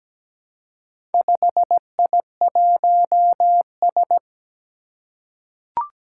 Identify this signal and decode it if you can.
Morse code, heard as a keyed tone, about 17 wpm, 700 Hz. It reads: 5I1S
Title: Four Challenging Letter-Number Combinations 17wpm